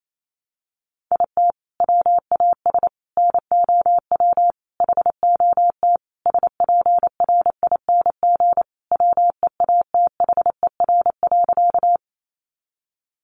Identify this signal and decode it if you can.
Morse code, heard as a keyed tone, about 28 wpm, 700 Hz. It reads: IT WAS NOW HOT SPRING WEATHER.
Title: Sentences from Top 1200 Words 28wpm